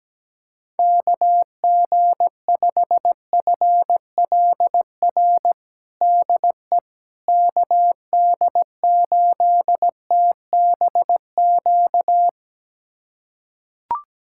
Morse code, heard as a keyed tone, about 17 wpm, 700 Hz. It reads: KG5FLR DE KD8TBQ
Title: USA Callsign DE Callsign 17wpm